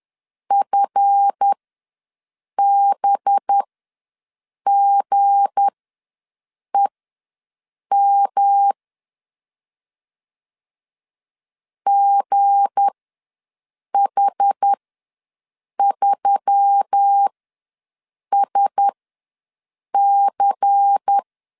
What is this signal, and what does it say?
Morse code, heard as a keyed tone, about 10 wpm, spaced as for 6 wpm, 800 Hz. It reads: FBGEM GH3SC
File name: LEZIONI / 9 286 ws 10wpm 4032